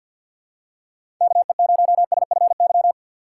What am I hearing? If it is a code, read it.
Morse code, heard as a keyed tone, about 52 wpm, 700 Hz. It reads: KE0SFY
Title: USA Call Signs - 6 Characters - 3x Character Spacing - 50wpm